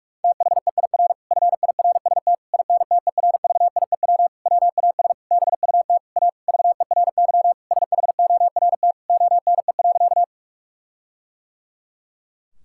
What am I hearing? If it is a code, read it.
Morse code, heard as a keyed tone, about 42 wpm, 700 Hz. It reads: THEIR FIRST INTERVIEW WAS BUT A VERY SHORT ONE.
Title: Sentences from Top 1600 Words 45wpm